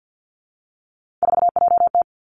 Morse code, heard as a keyed tone, about 50 wpm, 700 Hz. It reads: 4JT
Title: Three Challenging Letter-Number Combinations 50wpm